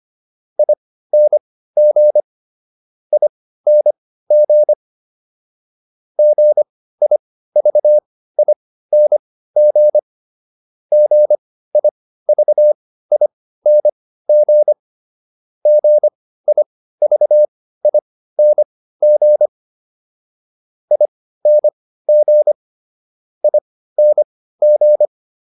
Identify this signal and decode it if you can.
Morse code, heard as a keyed tone, about 25 wpm, spaced as for 15 wpm, 600 Hz. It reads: ING ING GIVING GIVING GIVING ING ING